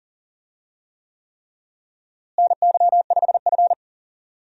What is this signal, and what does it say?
Morse code, heard as a keyed tone, about 40 wpm, 700 Hz. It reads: NY5F
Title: USA Call Signs - 4 Characters - Repeated 3x - 40wpm